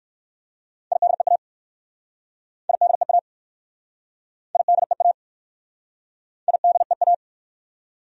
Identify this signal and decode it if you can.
Morse code, heard as a keyed tone, about 60 wpm, 700 Hz. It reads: IDEA IDEA IDEA IDEA